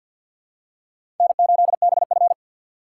Morse code, heard as a keyed tone, about 53 wpm, 700 Hz. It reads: N8BF
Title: USA Call Signs - 4x Character Spacing - 50wpm